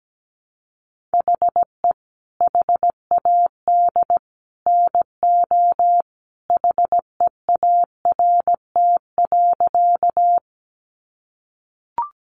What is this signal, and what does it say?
Morse code, heard as a keyed tone, about 17 wpm, 700 Hz. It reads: HE HAD NO HEART.
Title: Sentences from Top 1200 Words 17wpm